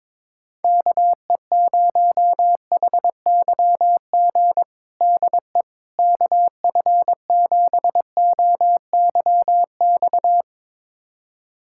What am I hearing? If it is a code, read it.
Morse code, heard as a keyed tone, about 22 wpm, 700 Hz. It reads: KE0HYG DE KF7OYX
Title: USA Callsign DE Callsign - 6 Characters - 22wpm